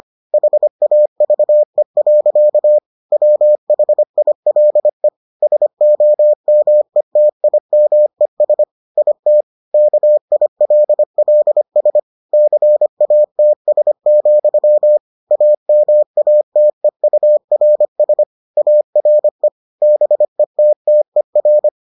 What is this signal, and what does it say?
Morse code, heard as a keyed tone, about 25 wpm, 600 Hz. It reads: HAVE. WHILE SOMETIMES IT KILLS CATS, AMATEURS ARE BETTER